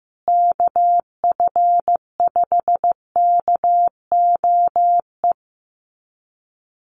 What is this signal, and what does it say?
Morse code, heard as a keyed tone, about 15 wpm, 700 Hz. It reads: KF5KOE
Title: USA Call Signs Repeated 3x 15wpm